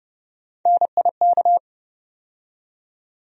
Morse code, heard as a keyed tone, about 30 wpm, 700 Hz. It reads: NIK
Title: USA Call Sign Postfixes - 3 Letters 30wpm